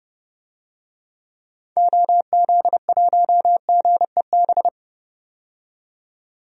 Morse code, heard as a keyed tone, about 32 wpm, 700 Hz. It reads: OZ1GEB
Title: DXCC-100 Call Signs - 6 Characters - 30wpm